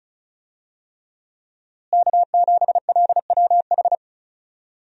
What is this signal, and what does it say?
Morse code, heard as a keyed tone, about 36 wpm, 700 Hz. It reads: K7LWH
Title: USA Call Signs - 5 Characters - Repeated 3x - 35wpm